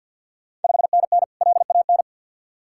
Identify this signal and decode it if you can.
Morse code, heard as a keyed tone, about 51 wpm, 690 Hz. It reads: 5NN LAN
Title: Ontario QSO Party Exchange 50wpm